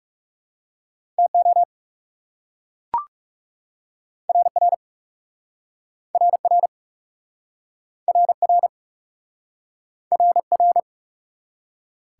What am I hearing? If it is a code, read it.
Morse code, heard as a keyed tone, about 37 wpm, 700 Hz. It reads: TO RR RR RR RR
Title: Speed-Racing Top 180 QSO Elements 30wpm